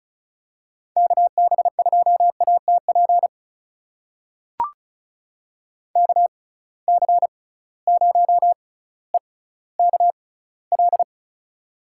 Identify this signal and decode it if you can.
Morse code, heard as a keyed tone, about 36 wpm, spaced as for 18 wpm, 700 Hz. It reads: KB2ATP KC0EKL
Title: USA Call Signs - 6 Characters - 6x Character Spacing - 35wpm